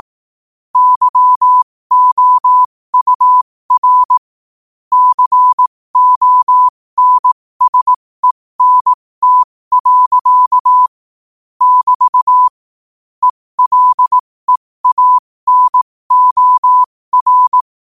Morse code, heard as a keyed tone, about 18 wpm, spaced as for 15 wpm, 1000 Hz. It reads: YOUR CONSENT. <BT> ELEANOR
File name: qotdfiles / 15wpm